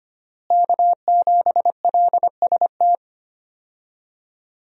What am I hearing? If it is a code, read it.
Morse code, heard as a keyed tone, about 25 wpm, 700 Hz. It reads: K7LST
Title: USA Call Signs - 5 Characters - Repeated 2x - 25wpm